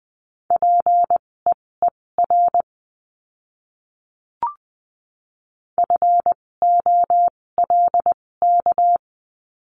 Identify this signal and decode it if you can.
Morse code, heard as a keyed tone, about 20 wpm, spaced as for 16 wpm, 700 Hz. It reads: PEER FOLK
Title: 256 Lesson 42a - Words 12-20wpm